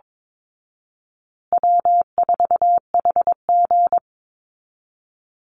Morse code, heard as a keyed tone, about 22 wpm, 700 Hz. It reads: W4HG
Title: Easy Call Signs Repeated 2x - 22wpm